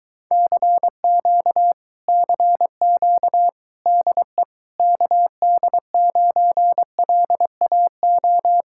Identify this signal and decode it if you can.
Morse code, heard as a keyed tone, about 23 wpm, 700 Hz. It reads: CQ CQ DE KD9LAO